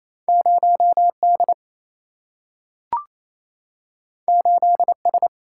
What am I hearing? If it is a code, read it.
Morse code, heard as a keyed tone, about 28 wpm, 700 Hz. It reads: 0D 8S
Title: Single-Letter Single-Number 28wpm